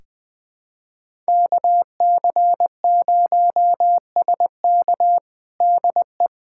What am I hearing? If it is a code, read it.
Morse code, heard as a keyed tone, about 20 wpm, 700 Hz. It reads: KC0SK DE
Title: USA Callsign DE Callsign 20wpm